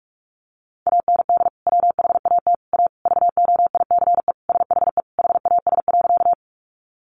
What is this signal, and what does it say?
Morse code, heard as a keyed tone, about 44 wpm, 700 Hz. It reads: AND WHAT A VOICE SHE HAS.
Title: Sentences from Top 500 Words 45wpm